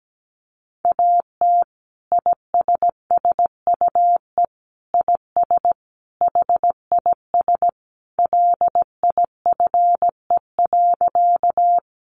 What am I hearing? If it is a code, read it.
Morse code, heard as a keyed tone, about 17 wpm, 700 Hz. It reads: AT ISSUE IS HIS LIFE.
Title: Sentences from Top 900 Words 17wpm